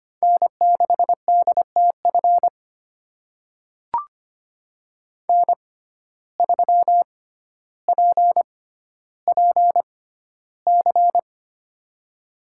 Morse code, heard as a keyed tone, about 25 wpm, spaced as for 13 wpm, 700 Hz. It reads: N6DTF N3PPC
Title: USA Call Signs - 5 Characters - 6x Character Spacing - 25wpm